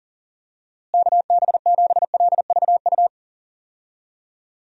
Morse code, heard as a keyed tone, about 38 wpm, 700 Hz. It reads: KB7LVU
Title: USA Call Signs - 6 Characters - Repeated 3x - 40wpm